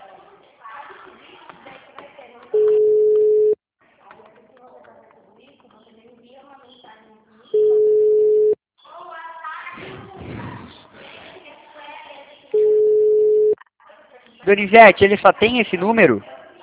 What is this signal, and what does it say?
Ringback tone (Europe, 425 Hz), with speech.
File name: out- -2009- - - .29367